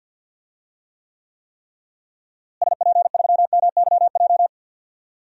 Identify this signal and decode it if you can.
Morse code, heard as a keyed tone, about 50 wpm, 700 Hz. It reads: SP3MYJ
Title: DXCC-100 Call Signs - 6 Characters - Repeated 3x - 50wpm